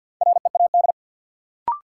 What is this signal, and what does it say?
Morse code, heard as a keyed tone, about 53 wpm, 700 Hz. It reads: READ
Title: Sentences from Top 200 Words - New Word Drill - 50wpm